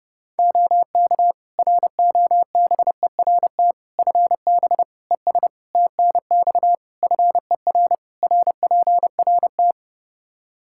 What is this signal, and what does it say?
Morse code, heard as a keyed tone, about 30 wpm, 700 Hz. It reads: OK ROBERT FB ES TNX FER RPRT
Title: K1USN SST QSO Practice 30wpm